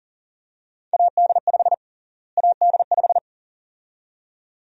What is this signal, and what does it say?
Morse code, heard as a keyed tone, about 40 wpm, 700 Hz. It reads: AD5 AD5
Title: USA Call Sign Prefixes - Repeated 2x - 40wpm